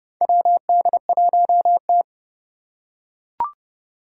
Morse code, heard as a keyed tone, about 29 wpm, 700 Hz. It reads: WD1T
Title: Four Challenging Letter-Number Combinations 30wpm